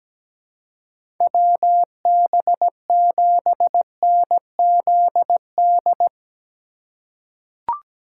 Morse code, heard as a keyed tone, about 17 wpm, 700 Hz. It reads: WB7NZD
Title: USA Call Signs - 3x Character Spacing - 17wpm